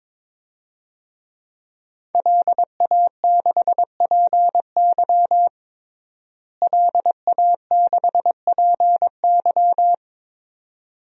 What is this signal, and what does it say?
Morse code, heard as a keyed tone, about 22 wpm, 700 Hz. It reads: LA6PY LA6PY
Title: DXCC-100 Call Signs - 5 Characters - Repeated 2x - 22wpm